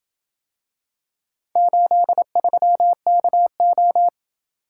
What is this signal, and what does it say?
Morse code, heard as a keyed tone, about 27 wpm, 700 Hz. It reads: 83KO